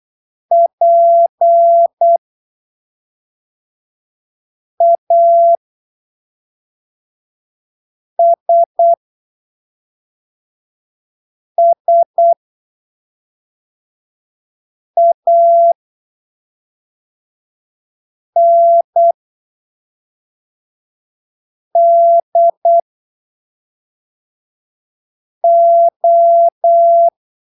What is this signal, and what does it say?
Morse code, heard as a keyed tone, about 8 wpm, spaced as for 3 wpm, 680 Hz. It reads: PASSANDO
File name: arquivos / texto / texto 26 08ppm